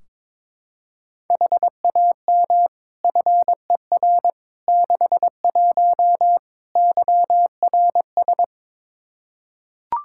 Morse code, heard as a keyed tone, about 22 wpm, 700 Hz. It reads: HAM FER 61 YRS
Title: Practice QSO - Exchange 3 - More Info - 22wpm